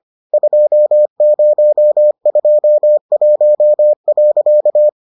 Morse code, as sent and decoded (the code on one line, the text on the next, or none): ..--- ----- ..--- .---- .-.-.-
2021.